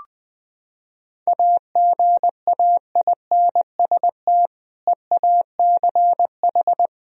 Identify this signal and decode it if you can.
Morse code, heard as a keyed tone, about 21 wpm, 700 Hz. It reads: AGAINST EACH
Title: Sentences from Top 1000 Words - Review Part 1 - 20wpm